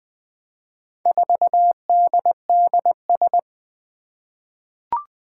Morse code, heard as a keyed tone, about 20 wpm, 700 Hz. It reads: 4DDS
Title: Four Challenging Letter-Number Combinations 20wpm